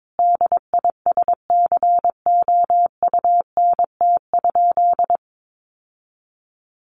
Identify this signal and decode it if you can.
Morse code, heard as a keyed tone, about 22 wpm, 700 Hz. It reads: DISCOUNT?
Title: Sentences from Top 1200 Words 22wpm